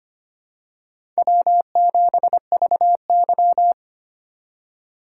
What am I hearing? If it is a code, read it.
Morse code, heard as a keyed tone, about 25 wpm, 700 Hz. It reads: W7VY
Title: USA Call Signs - 4 Characters - 2x Character Spacing - 25wpm